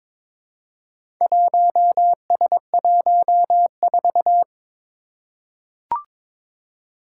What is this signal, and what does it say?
Morse code, heard as a keyed tone, about 21 wpm, 700 Hz. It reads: 1S14
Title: Four Challenging Letter-Number Combinations 22wpm